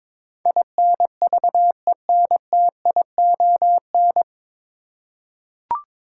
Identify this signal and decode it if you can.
Morse code, heard as a keyed tone, about 22 wpm, 700 Hz. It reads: INVENTION
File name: 9 Letter Words - First 3 Letters Given - 22wpm